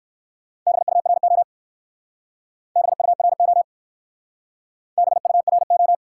Morse code, heard as a keyed tone, about 53 wpm, 700 Hz. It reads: BURO BURO BURO
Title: Speed-Racing Top 90 QSO Elements 45wpm